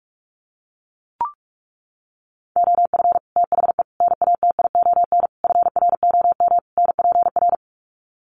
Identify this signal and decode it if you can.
Morse code, heard as a keyed tone, about 44 wpm, 700 Hz. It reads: OF THE NATION FROM NPR
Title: Sets of 5 Words 45wpm